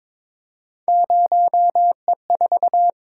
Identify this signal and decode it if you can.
Morse code, heard as a keyed tone, about 26 wpm, 700 Hz. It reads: 0E4
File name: Speed-Racing - Three Letter-Number Combinations - 22wpm